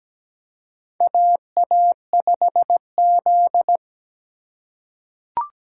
Morse code, heard as a keyed tone, about 17 wpm, 700 Hz. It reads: AA5Z
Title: USA Call Signs - 4 Characters - 4x Character Spacing - 17wpm